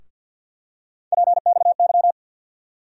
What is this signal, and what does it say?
Morse code, heard as a keyed tone, about 47 wpm, 700 Hz. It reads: PXY
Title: USA Call Sign Postfixes - 3 Letters 50wpm